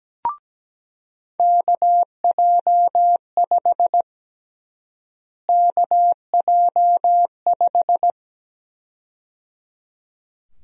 Morse code, heard as a keyed tone, about 17 wpm, 700 Hz. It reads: KJ5 KJ5
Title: USA Call Sign Prefixes - Repeated 2x - 17wpm